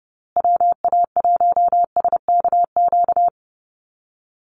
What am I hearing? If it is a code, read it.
Morse code, heard as a keyed tone, about 30 wpm, 700 Hz. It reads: WA1SKQ
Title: Call Signs Repeated 2x - 30wpm